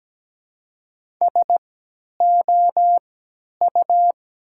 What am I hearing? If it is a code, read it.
Morse code, heard as a keyed tone, about 17 wpm, spaced as for 10 wpm, 700 Hz. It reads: SOU
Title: 3x Character Spacing - Top 100 Words - 17wpm